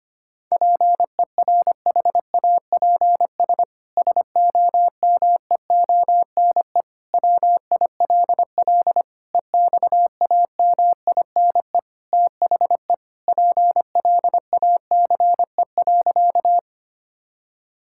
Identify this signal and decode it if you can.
Morse code, heard as a keyed tone, about 25 wpm, 700 Hz. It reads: PERHAPS SOMEONE WILL EXAMINE THE PLACE.
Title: Sentences from Top 1600 Words 25wpm